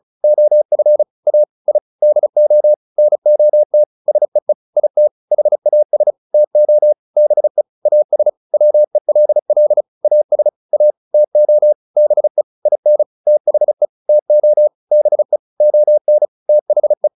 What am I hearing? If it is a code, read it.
Morse code, heard as a keyed tone, about 34 wpm, 600 Hz. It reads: OF A I DO NOT SEE IT HAS TO BE AS WELL AS A TO BE IN THE TO BE ON THE